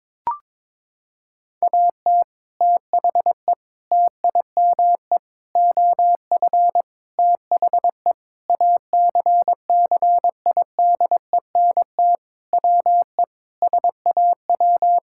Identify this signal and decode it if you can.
Morse code, heard as a keyed tone, about 21 wpm, 700 Hz. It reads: AT THE TIME OF THE ACCIDENT WE SAW
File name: practice / 22wpm / Sentences with 4-Word Phrases - 22wpm